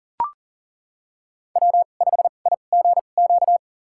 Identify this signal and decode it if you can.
Morse code, heard as a keyed tone, about 40 wpm, spaced as for 29 wpm, 700 Hz. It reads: W5IGQ
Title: USA Call Signs - 2x Character Spacing - 40wpm